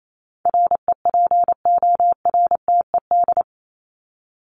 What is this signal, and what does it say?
Morse code, heard as a keyed tone, about 28 wpm, 700 Hz. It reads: REPORTED
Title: ED Suffix - Top 100 Words with Base Words 28wpm